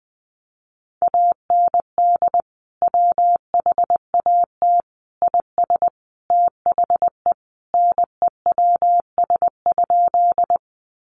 Morse code, heard as a keyed tone, about 20 wpm, 700 Hz. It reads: AND WHAT IS THE NEWS?
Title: Sentences from Top 800 Words 20wpm